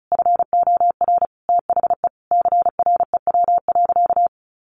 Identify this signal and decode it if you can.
Morse code, heard as a keyed tone, about 34 wpm, 700 Hz. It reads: FOR THE CREW.
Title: Sentences from Top 1500 Words 35wpm